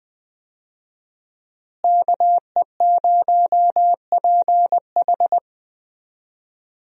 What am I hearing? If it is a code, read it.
Morse code, heard as a keyed tone, about 20 wpm, 700 Hz. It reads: KE0PH
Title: USA Call Signs - 5 Characters - Repeated 3x - 20wpm